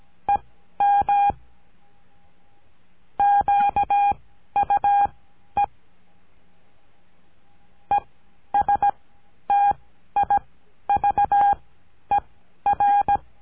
Morse code, heard as a keyed tone, about 17 wpm, spaced as for 11 wpm, 810 Hz. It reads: EM QUE ESTIVER
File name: CURSO DE TELEGRAFIA